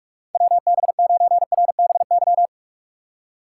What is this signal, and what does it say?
Morse code, heard as a keyed tone, about 47 wpm, 700 Hz. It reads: WB9RBY
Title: USA Call Signs - 2x Character Spacing - 45wpm